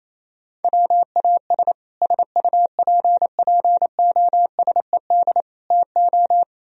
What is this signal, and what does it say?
Morse code, heard as a keyed tone, about 35 wpm, 700 Hz. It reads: WAS SUPPOSED TO
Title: Speed-Racing - Sets of 4 Words 28wpm